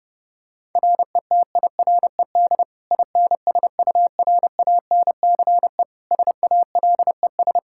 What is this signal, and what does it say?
Morse code, heard as a keyed tone, about 30 wpm, 700 Hz. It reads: RETIRED INSURANCE SALES
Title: Practice QSO - Exchange 3 - More Info - 30wpm